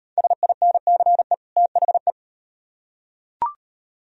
Morse code, heard as a keyed tone, about 38 wpm, 700 Hz. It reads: SINCE THE